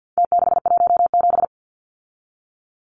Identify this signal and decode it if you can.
Morse code, heard as a keyed tone, about 50 wpm, 700 Hz. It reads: T617
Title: Numbers - Contest Style - 4-Digit 50wpm